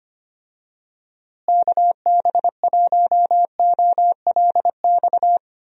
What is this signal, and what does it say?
Morse code, heard as a keyed tone, about 25 wpm, 700 Hz. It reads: KB1OLX
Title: USA Call Signs - 6 Characters - Repeated 2x - 25wpm